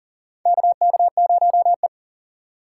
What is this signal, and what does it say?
Morse code, heard as a keyed tone, about 43 wpm, 690 Hz. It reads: KK0E
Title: USA Call Signs - 4 Characters - 6x Character Spacing - 40wpm